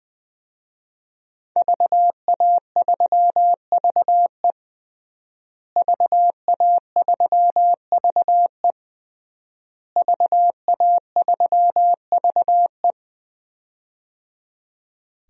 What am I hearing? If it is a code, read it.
Morse code, heard as a keyed tone, about 21 wpm, 700 Hz. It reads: VA3VE VA3VE VA3VE
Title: DXCC-100 Call Signs - 5 Characters - Repeated 3x - 20wpm